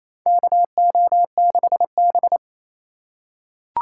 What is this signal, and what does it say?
Morse code, heard as a keyed tone, about 28 wpm, 700 Hz. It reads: KO6B
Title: USA Call Signs - 4 Characters - 4x Character Spacing - 28wpm